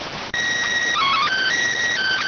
SSTV (slow-scan television), PD 290.